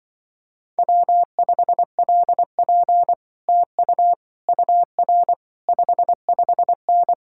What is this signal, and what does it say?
Morse code, heard as a keyed tone, about 24 wpm, 700 Hz. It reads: W5LP TU UR 55N
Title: Speed-Racing Common QSO Phrases - 20wpm